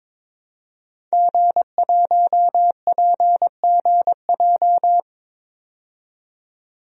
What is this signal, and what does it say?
Morse code, heard as a keyed tone, about 22 wpm, 700 Hz. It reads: G1PGJ
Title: DXCC-100 Call Signs - 5 Characters - 22wpm